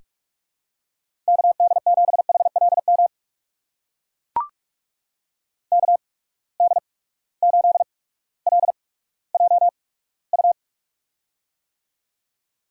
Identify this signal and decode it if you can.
Morse code, heard as a keyed tone, about 44 wpm, spaced as for 19 wpm, 700 Hz. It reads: KD7HLM KD8LJU
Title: USA Call Signs - 8x Character Spacing - 45wpm